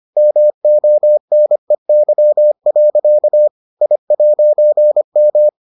Morse code, heard as a keyed tone, about 25 wpm, 600 Hz. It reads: MONEY. I'M